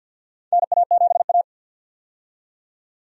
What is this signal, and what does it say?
Morse code, heard as a keyed tone, about 51 wpm, 700 Hz. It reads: NA7A
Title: USA Call Signs - 4 Characters - 50wpm